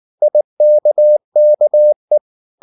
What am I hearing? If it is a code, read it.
Morse code, heard as a keyed tone, about 19 wpm, 600 Hz. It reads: IKKE